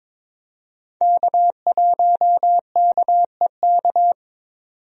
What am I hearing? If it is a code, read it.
Morse code, heard as a keyed tone, about 22 wpm, 700 Hz. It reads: K1KEK
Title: USA Call Signs - 5 Characters - 22wpm